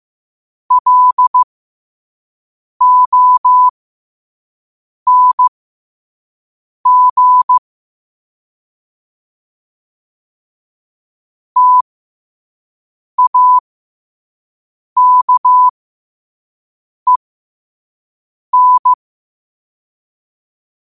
Morse code, heard as a keyed tone, about 15 wpm, spaced as for 5 wpm, 1000 Hz. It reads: LONG TAKEN
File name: qotdfiles / 5wpm